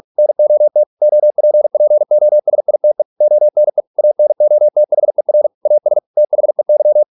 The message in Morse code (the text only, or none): NOT OPPOSITE ONE ANOTHER AS THEY